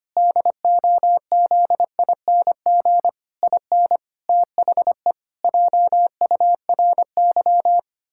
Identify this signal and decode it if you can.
Morse code, heard as a keyed tone, about 25 wpm, 700 Hz. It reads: DOZING IN THE JURY
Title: Pangram Challenge v2 - 25wpm